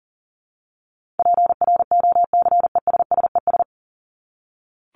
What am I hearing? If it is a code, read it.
Morse code, heard as a keyed tone, about 40 wpm, 700 Hz. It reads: PROCESSES
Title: ES:S Suffix - Top 100 Words with Base Words 40wpm